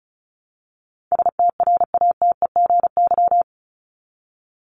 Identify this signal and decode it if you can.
Morse code, heard as a keyed tone, about 34 wpm, 700 Hz. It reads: STRATEGY
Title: Top 501-750 Words 35wpm